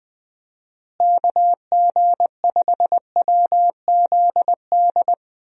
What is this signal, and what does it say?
Morse code, heard as a keyed tone, about 20 wpm, 700 Hz. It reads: KG5WZD
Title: USA Call Signs - 6 Characters - Repeated 2x - 20wpm